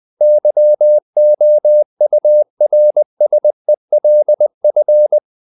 Morse code, heard as a keyed tone, about 20 wpm, 600 Hz. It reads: YOURSELF
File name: CW / Practice / If20